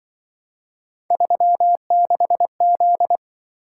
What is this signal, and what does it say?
Morse code, heard as a keyed tone, about 24 wpm, 700 Hz. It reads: 36Z
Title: Speed-Racing - Three Letter-Number Combinations - 20wpm